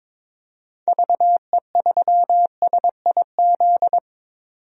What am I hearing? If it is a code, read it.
Morse code, heard as a keyed tone, about 22 wpm, 700 Hz. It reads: VE3SIZ
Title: DXCC-100 Call Signs - 6 Characters - 22wpm